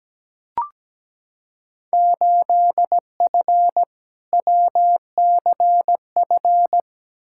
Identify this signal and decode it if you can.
Morse code, heard as a keyed tone, about 17 wpm, 700 Hz. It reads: 8F WCF
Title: Field Day Exchanges Encore - 17wpm